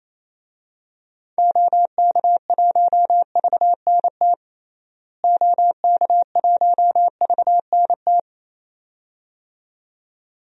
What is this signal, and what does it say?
Morse code, heard as a keyed tone, about 28 wpm, 700 Hz. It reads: OK1VNT OK1VNT
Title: DXCC-100 Call Signs - 6 Characters - Repeated 2x - 28wpm